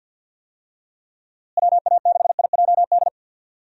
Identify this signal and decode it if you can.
Morse code, heard as a keyed tone, about 48 wpm, 700 Hz. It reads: WA6IJD
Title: USA Call Signs - 6 Characters - Repeated 3x - 50wpm